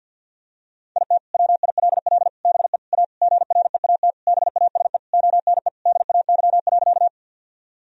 Morse code, heard as a keyed tone, about 52 wpm, 700 Hz. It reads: IT WILL BE A GREAT BASE ONE DAY.
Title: Sentences from Top 1000 Words - Review Part 1 - 50wpm